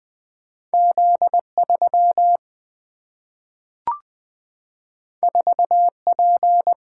Morse code, heard as a keyed tone, about 20 wpm, 700 Hz. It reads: Z3 4P
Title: Single-Letter Single-Number 20wpm